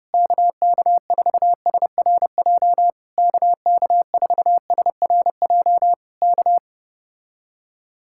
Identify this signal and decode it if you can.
Morse code, heard as a keyed tone, about 32 wpm, 700 Hz. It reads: KK4SRJ KK4SRJ K